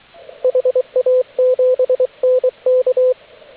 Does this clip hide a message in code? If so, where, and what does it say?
Yes: Morse code from 0.4 to 3.1 s, heard as a keyed tone, about 23 wpm, 510 Hz, with radio noise. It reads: HA7NK